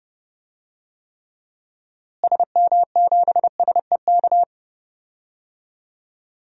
Morse code, heard as a keyed tone, about 30 wpm, 700 Hz. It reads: SM7SEK
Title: DXCC-100 Call Signs - 6 Characters - Repeated 3x - 30wpm